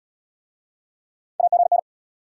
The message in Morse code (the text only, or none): AGN